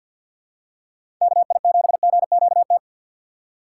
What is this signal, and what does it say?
Morse code, heard as a keyed tone, about 50 wpm, 700 Hz. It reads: KI7GQT
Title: USA Call Signs - 6 Characters - 6x Character Spacing - 50wpm